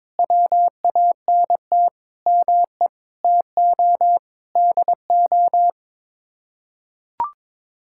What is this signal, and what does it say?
Morse code, heard as a keyed tone, about 22 wpm, 700 Hz. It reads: WANT ME TO DO
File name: practice / 22wpm / Sets of 7 Words - 22wpm